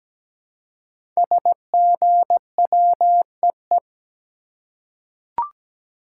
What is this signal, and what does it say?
Morse code, heard as a keyed tone, about 17 wpm, 700 Hz. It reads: SGWEE
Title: Code Groups - 5 Letters - 17wpm